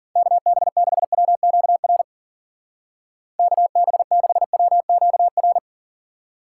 Morse code, heard as a keyed tone, about 50 wpm, 700 Hz. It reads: KB6WQR KB6WQR